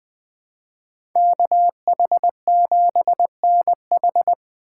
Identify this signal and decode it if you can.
Morse code, heard as a keyed tone, about 20 wpm, 700 Hz. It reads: KH7NH
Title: USA Call Signs - 5 Characters - 2x Character Spacing - 20wpm